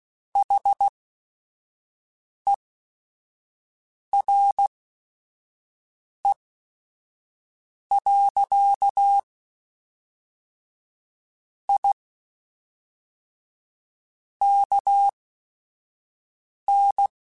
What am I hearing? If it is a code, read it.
Morse code, heard as a keyed tone, about 16 wpm, spaced as for 5 wpm, 800 Hz. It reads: HERE. I KN